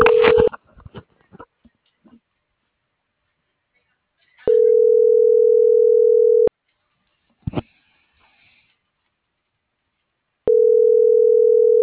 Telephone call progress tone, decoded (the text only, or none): ringback tone